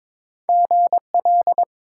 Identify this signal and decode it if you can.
Morse code, heard as a keyed tone, about 21 wpm, 700 Hz. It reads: GL
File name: Top 90 QSO Elements - 3x Character Spacing - 22wpm